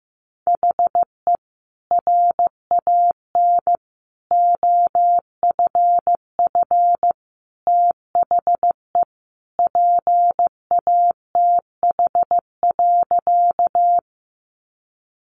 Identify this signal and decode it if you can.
Morse code, heard as a keyed tone, about 15 wpm, 700 Hz. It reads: HE RAN OFF THE PATH.
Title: A Sound Of Thunder 15wpm